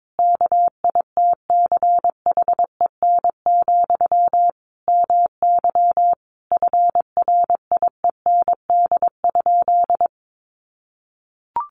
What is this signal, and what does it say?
Morse code, heard as a keyed tone, about 22 wpm, 700 Hz. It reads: KITCHEN, MY FRIEND?
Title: Sentences from Top 1400 Words 22wpm